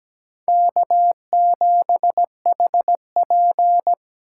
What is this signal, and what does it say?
Morse code, heard as a keyed tone, about 17 wpm, 700 Hz. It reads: K7HP
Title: USA Call Signs - 4 Characters - 4x Character Spacing - 17wpm